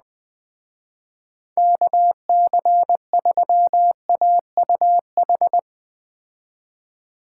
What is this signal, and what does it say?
Morse code, heard as a keyed tone, about 20 wpm, 700 Hz. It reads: KC3AUH